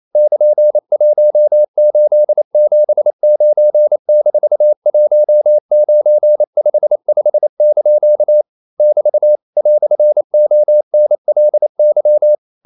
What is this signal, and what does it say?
Morse code, heard as a keyed tone, about 28 wpm, 600 Hz. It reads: <KN>1879-1955) <BT> "ONLY
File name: CW / Practice / EnglishQuotations0128